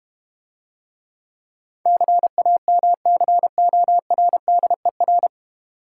Morse code, heard as a keyed tone, about 32 wpm, 700 Hz. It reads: CAMCORDER